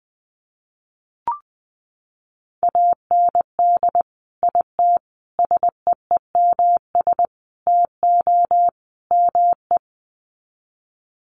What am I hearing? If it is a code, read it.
Morse code, heard as a keyed tone, about 20 wpm, 700 Hz. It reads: AND IT SEEMS TO ME
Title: Sets of 5 Words Encore2 20wpm